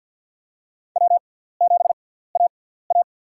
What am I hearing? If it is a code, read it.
Morse code, heard as a keyed tone, about 53 wpm, spaced as for 17 wpm, 700 Hz. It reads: W7AA
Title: USA Call Signs - 4 Characters - 6x Character Spacing - 50wpm